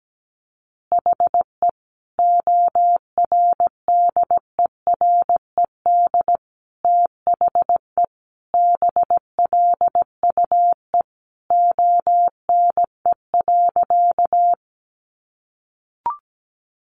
Morse code, heard as a keyed tone, about 17 wpm, 700 Hz. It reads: HE ORDERED THE BLUE ONE.
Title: Additional Easy Phrases 17wpm